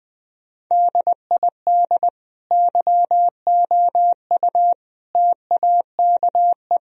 Morse code, heard as a keyed tone, about 21 wpm, 700 Hz. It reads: DID YOU TAKE
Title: Sentences from Top 1000 Words - Review Part 1 - 20wpm